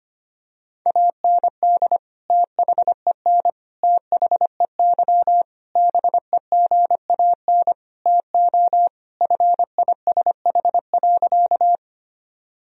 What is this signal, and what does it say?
Morse code, heard as a keyed tone, about 25 wpm, 700 Hz. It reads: AND THEN THEY BEGAN TO FISH.
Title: Sentences from Top 1000 Words - Review Part 1 - 25wpm